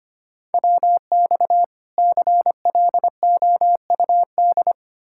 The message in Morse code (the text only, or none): WX CLOUD